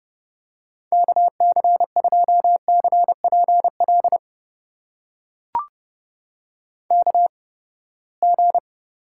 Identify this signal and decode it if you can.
Morse code, heard as a keyed tone, about 32 wpm, spaced as for 21 wpm, 700 Hz. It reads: KC2CPL KG
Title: USA Call Signs - 6 Characters - 8x Character Spacing - 30wpm